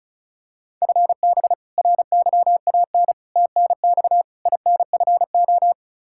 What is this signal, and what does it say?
Morse code, heard as a keyed tone, about 36 wpm, 700 Hz. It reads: FB RYAN TNX INFO
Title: Practice QSO - Exchange 3 - More Info - 35wpm